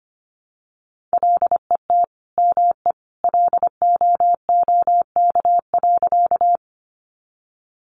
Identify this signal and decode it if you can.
Morse code, heard as a keyed tone, about 25 wpm, 700 Hz. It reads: LET ME LOOK.
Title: Sentences from Top 300 Words 25wpm